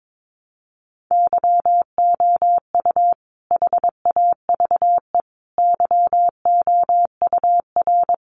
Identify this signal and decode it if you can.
Morse code, heard as a keyed tone, about 22 wpm, 700 Hz. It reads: YOU HAVE YOUR
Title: Sentences from Top 400 Words 22wpm